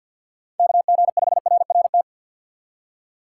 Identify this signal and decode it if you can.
Morse code, heard as a keyed tone, about 61 wpm, 700 Hz. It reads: KG5RRT
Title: Speed Racing - USA Call Signs 50wpm